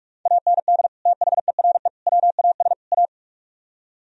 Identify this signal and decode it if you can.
Morse code, heard as a keyed tone, about 46 wpm, 700 Hz. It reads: AND THERE WAS A